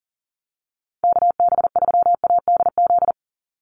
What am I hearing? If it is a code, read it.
Morse code, heard as a keyed tone, about 40 wpm, 700 Hz. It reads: KB3ADZ
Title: USA Call Signs 40wpm